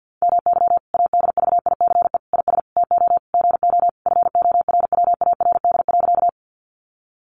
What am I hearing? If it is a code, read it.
Morse code, heard as a keyed tone, about 50 wpm, 700 Hz. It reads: MY ADVICE IS TO GO FORWARD.
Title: Sentences from Top 1300 Words 50wpm